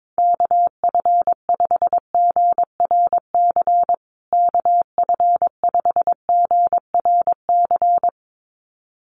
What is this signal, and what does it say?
Morse code, heard as a keyed tone, about 22 wpm, 700 Hz. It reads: KF5GRC KF5GRC
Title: USA Call Signs Repeated 2x 22wpm